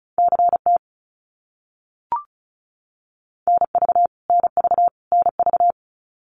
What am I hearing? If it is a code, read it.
Morse code, heard as a keyed tone, about 34 wpm, 700 Hz. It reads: CT NV NV NV
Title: US States Repeated 3x 35wpm